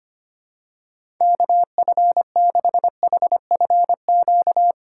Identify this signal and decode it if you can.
Morse code, heard as a keyed tone, about 25 wpm, 700 Hz. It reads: KF6HFQ
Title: USA Call Signs - 6 Characters - 4x Character Spacing - 25wpm